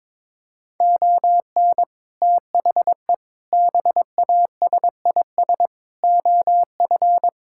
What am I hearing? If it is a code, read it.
Morse code, heard as a keyed tone, about 22 wpm, 700 Hz. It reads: ON THE BASIS OF
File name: practice / 22wpm / Sentences with 4-Word Phrases - 22wpm